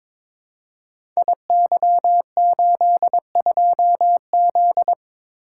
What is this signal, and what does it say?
Morse code, heard as a keyed tone, about 22 wpm, 700 Hz. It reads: IY82Z
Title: Code Groups - 5 Letter:Numbers - 22wpm